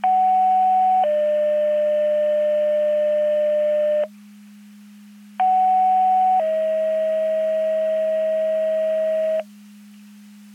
Two-tone page, heard as a one-second tone, then a three-second tone, about 750 then 600 Hz, about 750 then 640 Hz. It is heard from 0.0 to 9.4 s.
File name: Portage 154.385 Paging